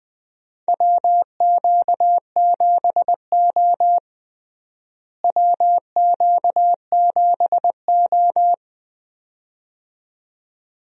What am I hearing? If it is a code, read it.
Morse code, heard as a keyed tone, about 20 wpm, 700 Hz. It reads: WQ7O WQ7O
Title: USA Call Signs - 4 Characters - Repeated 2x - 20wpm